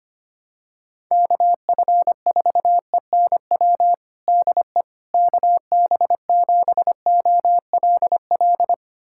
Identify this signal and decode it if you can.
Morse code, heard as a keyed tone, about 25 wpm, 700 Hz. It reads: KF4ENW DE KB7OLL
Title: USA Callsign DE Callsign - 6 Characters - 25wpm